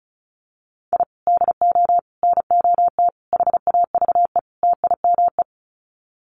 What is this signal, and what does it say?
Morse code, heard as a keyed tone, about 34 wpm, 700 Hz. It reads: I DO NOT HAVE TIME
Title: Sets of 5 Words Encore 35wpm